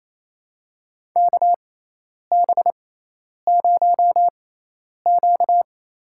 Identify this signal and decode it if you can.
Morse code, heard as a keyed tone, about 28 wpm, spaced as for 10 wpm, 700 Hz. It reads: KB0Q
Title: USA Call Signs - 6 Characters - 6x Character Spacing - 28wpm